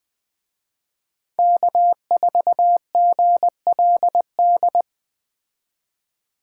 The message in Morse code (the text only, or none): K4GLD